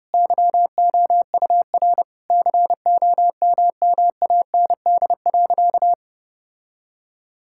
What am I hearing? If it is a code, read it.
Morse code, heard as a keyed tone, about 30 wpm, 700 Hz. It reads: YOUR COMMAND.
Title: Sentences from Top 1600 Words 30wpm